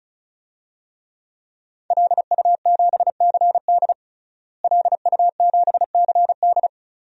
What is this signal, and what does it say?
Morse code, heard as a keyed tone, about 36 wpm, 700 Hz. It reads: LU7CD LU7CD
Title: DXCC-100 Call Signs - 5 Characters - Repeated 3x - 35wpm